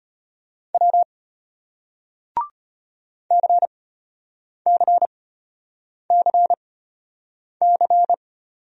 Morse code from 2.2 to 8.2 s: CCCC

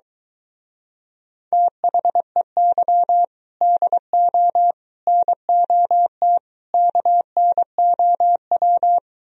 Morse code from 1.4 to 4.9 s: THEY DO

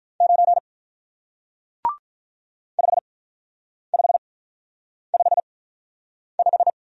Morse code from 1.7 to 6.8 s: 5555

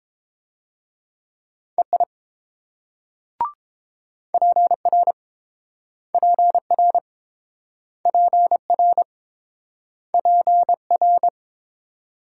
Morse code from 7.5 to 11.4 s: PR PR

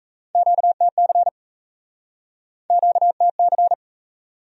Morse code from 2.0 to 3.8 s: QTC

